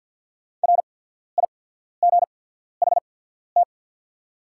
Morse code, heard as a keyed tone, about 48 wpm, spaced as for 14 wpm, 690 Hz. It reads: RIGHT
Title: 8x Character Spacing - Top 200 Words - 50wpm